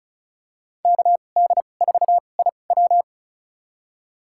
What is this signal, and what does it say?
Morse code, heard as a keyed tone, about 36 wpm, spaced as for 25 wpm, 700 Hz. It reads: KD4IW
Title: USA Call Signs - 5 Characters - 2x Character Spacing - 35wpm